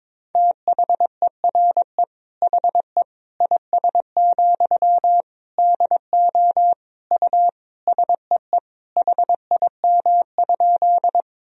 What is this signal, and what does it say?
Morse code, heard as a keyed tone, about 22 wpm, 700 Hz. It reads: THERE HE IS, DO U SEE HIM?
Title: Sentences from Top 100 Words and Abbreviations 22wpm